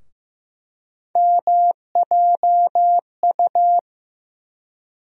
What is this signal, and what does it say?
Morse code, heard as a keyed tone, about 15 wpm, 700 Hz. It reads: MJU
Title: USA Call Sign Postfixes - 3 Letters 15wpm